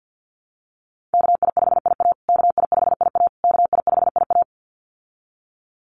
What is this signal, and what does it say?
Morse code, heard as a keyed tone, about 50 wpm, 700 Hz. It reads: KI5IA KI5IA KI5IA
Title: USA Call Signs Repeated 3x 50wpm